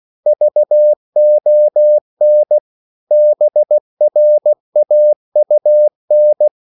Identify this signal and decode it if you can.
Morse code, heard as a keyed tone, about 16 wpm, 600 Hz. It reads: VON BRAUN
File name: CW / Practice / EnglishQuotations0416